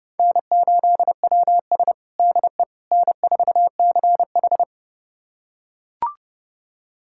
Morse code, heard as a keyed tone, about 30 wpm, 700 Hz. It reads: N8WS DE N4CH
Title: USA Callsign DE Callsign - 4 Characters - 30wpm